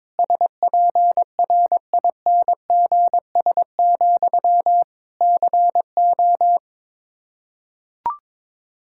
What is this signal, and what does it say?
Morse code, heard as a keyed tone, about 22 wpm, 700 Hz. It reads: SPRINGS, CO
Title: Practice QSO - Exchange 1 - Beginning - 22wpm